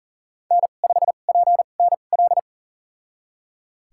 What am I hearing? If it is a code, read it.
Morse code, heard as a keyed tone, about 39 wpm, spaced as for 28 wpm, 700 Hz. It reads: N5PNL
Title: USA Call Signs - 2x Character Spacing - 40wpm